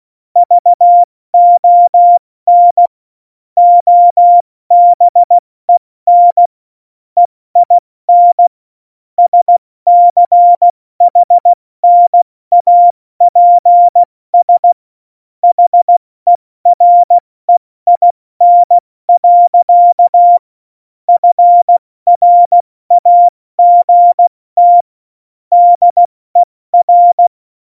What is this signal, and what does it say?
Morse code, heard as a keyed tone, about 16 wpm, 700 Hz. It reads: VON OBEN EIN SCHNAPS HEREIN. FRAGT DER
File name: Morsekurs / Witze / Witz 03. November 2020, 70 F80